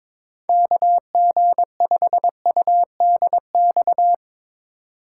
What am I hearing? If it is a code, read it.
Morse code, heard as a keyed tone, about 22 wpm, 700 Hz. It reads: KG5UDX
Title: USA Call Signs - 6 Characters - 3x Character Spacing - 22wpm